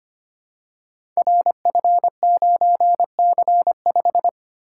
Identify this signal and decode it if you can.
Morse code, heard as a keyed tone, about 25 wpm, 700 Hz. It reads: RF9C5
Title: Code Groups - 5 Letter:Numbers - 25wpm